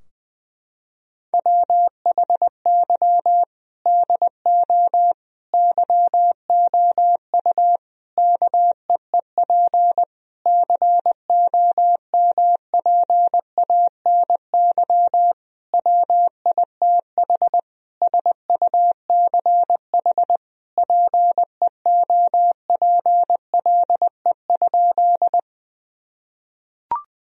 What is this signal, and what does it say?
Morse code, heard as a keyed tone, about 20 wpm, 700 Hz. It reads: WHY DO YOU KEEP COMPANY WITH SUCH PEOPLE?
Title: Sentences from Top 1000 Words - Review Part 3 - 20wpm